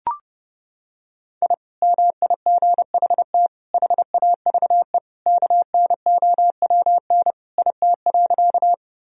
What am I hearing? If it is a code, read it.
Morse code, heard as a keyed tone, about 30 wpm, 700 Hz. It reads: I MIGHT HAVE KNOWN IT.
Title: Sentences from Top 1000 Words - Review Part 3 - 30wpm